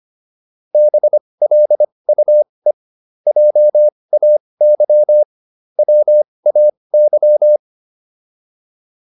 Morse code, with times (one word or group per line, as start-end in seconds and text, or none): 0.7-2.7 BLUE
3.3-5.2 JAY
5.8-7.6 WAY